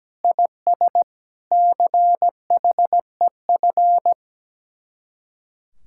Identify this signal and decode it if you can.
Morse code, heard as a keyed tone, about 17 wpm, 700 Hz. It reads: IS CHEF
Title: Speed-Racing Common QSO Phrases - 17wpm